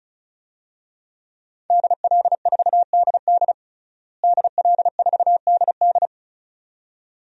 Morse code, heard as a keyed tone, about 36 wpm, 700 Hz. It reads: DL4DD DL4DD
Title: DXCC-100 Call Signs - 5 Characters - Repeated 2x - 35wpm